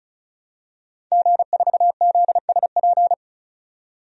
Morse code, heard as a keyed tone, about 34 wpm, 700 Hz. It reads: G4ZSP
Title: DXCC-100 Call Signs - 5 Characters - 35wpm